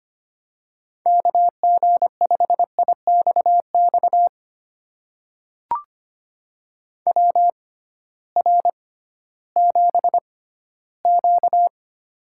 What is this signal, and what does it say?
Morse code, heard as a keyed tone, about 25 wpm, spaced as for 15 wpm, 700 Hz. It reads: KG5IXX WR7Q
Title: USA Call Signs - 6x Character Spacing - 25wpm